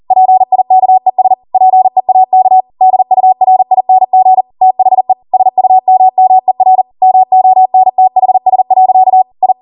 Morse code, heard as a keyed tone, about 40 wpm, 750 Hz. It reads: PIKES PEAK DURING THE SUMMER MONTHS. I